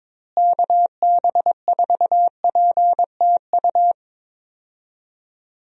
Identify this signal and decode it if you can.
Morse code, heard as a keyed tone, about 22 wpm, 700 Hz. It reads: KB4PTU
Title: USA Call Signs - 6 Characters - Repeated 2x - 22wpm